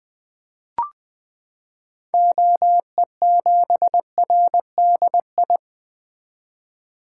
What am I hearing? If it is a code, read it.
Morse code, heard as a keyed tone, about 20 wpm, 700 Hz. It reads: OE7RDI
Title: 2023 SOTA Activator Call Signs - 20wpm